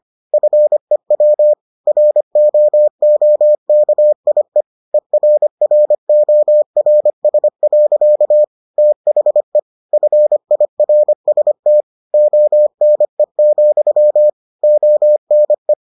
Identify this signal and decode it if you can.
Morse code, heard as a keyed tone, about 25 wpm, 600 Hz. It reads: FEW ROOKIE ERRORS. THE FIRST ONE, ONE